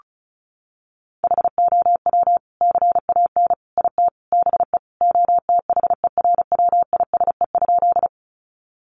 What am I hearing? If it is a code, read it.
Morse code, heard as a keyed tone, about 34 wpm, 700 Hz. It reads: HOW CAN IT BE OTHERWISE?
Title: Sentences from Top 1200 Words 35wpm